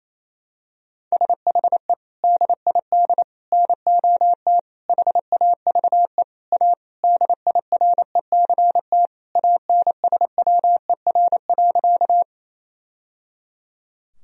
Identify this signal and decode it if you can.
Morse code, heard as a keyed tone, about 27 wpm, 700 Hz. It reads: SHE DID NOT HAVE A DIRECT ANSWER.
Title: Sentences from Top 1000 Words - Review Part 2 - 28wpm